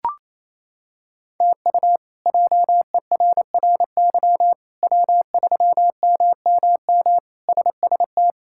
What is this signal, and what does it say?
Morse code, heard as a keyed tone, about 28 wpm, 700 Hz. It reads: TU JERRY W3MMM SST
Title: K1USN SST QSO Practice - 28wpm